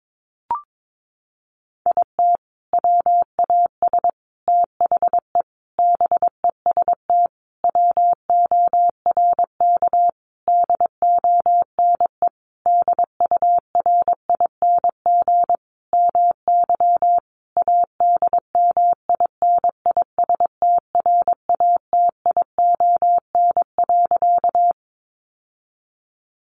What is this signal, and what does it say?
Morse code, heard as a keyed tone, about 22 wpm, 700 Hz. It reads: IT WAS THE BEST WORK DONE DURING MY ADMINISTRATION.
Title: Sentences from Top 1300 Words 22wpm